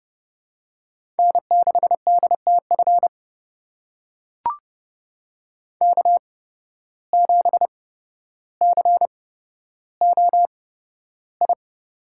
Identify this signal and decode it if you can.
Morse code, heard as a keyed tone, about 30 wpm, spaced as for 13 wpm, 700 Hz. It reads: N6DTF K7COI
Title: USA Call Signs - 5 Characters - 8x Character Spacing - 30wpm